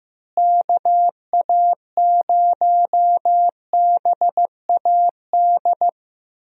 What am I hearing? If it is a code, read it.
Morse code, heard as a keyed tone, about 15 wpm, 700 Hz. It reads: KA0BAD
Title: USA Call Signs - 6 Characters - Repeated 2x - 15wpm